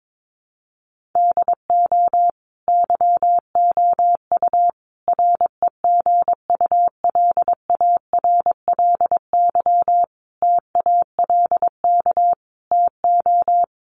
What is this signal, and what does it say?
Morse code, heard as a keyed tone, about 22 wpm, 700 Hz. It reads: DO YOU REGULARLY TALK TO
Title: Sentences from Top 1400 Words 22wpm